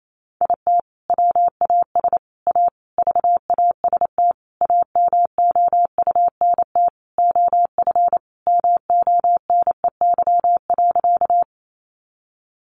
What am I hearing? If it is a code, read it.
Morse code, heard as a keyed tone, about 28 wpm, 700 Hz. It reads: IT WAS A VAST AMOUNT OF MONEY.
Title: Sentences from Top 1300 Words 28wpm